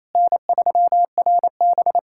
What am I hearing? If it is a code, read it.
Morse code, heard as a keyed tone, about 28 wpm, 700 Hz. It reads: N3RB
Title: SOTA QSO Practice 28wpm